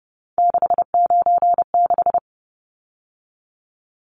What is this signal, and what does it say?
Morse code, heard as a keyed tone, about 30 wpm, 700 Hz. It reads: 696
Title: Numbers - 3-Digit 30wpm